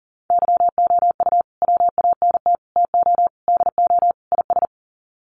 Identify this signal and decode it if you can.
Morse code, heard as a keyed tone, about 40 wpm, 700 Hz. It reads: YOU WANT TO DO IS
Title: Sets of 5 Words Encore2 40wpm